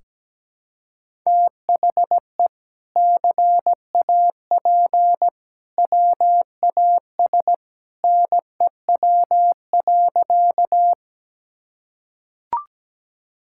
Morse code, heard as a keyed tone, about 17 wpm, 700 Hz. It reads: THE CAP WAS NEW.
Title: Sentences from Top 1600 Words 17wpm